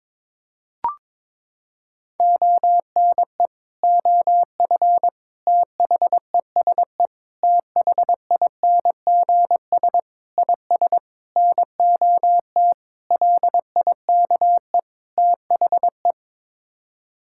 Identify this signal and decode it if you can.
Morse code, heard as a keyed tone, about 22 wpm, 700 Hz. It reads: ONE OF THESE THINGS IS NOT LIKE THE
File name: practice / 22wpm / Sets of 8 Words - 22wpm